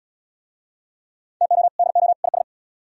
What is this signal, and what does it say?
Morse code, heard as a keyed tone, about 67 wpm, 700 Hz. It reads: TO DO IS